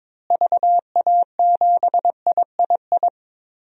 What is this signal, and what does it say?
Morse code, heard as a keyed tone, about 22 wpm, 700 Hz. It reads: VA7III
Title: DXCC-100 Call Signs - 6 Characters - Repeated 3x - 22wpm